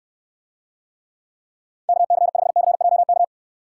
Morse code, heard as a keyed tone, about 69 wpm, 700 Hz. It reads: KC5CJK